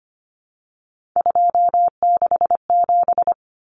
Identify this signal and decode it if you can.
Morse code, heard as a keyed tone, about 25 wpm, 700 Hz. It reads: 267
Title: Common Cut Numbers - 3-Digit 25wpm